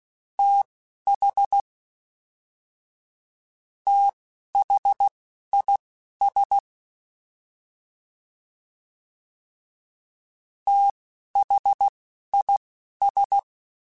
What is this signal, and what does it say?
Morse code, heard as a keyed tone, about 16 wpm, spaced as for 11 wpm, 800 Hz. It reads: TH THIS THIS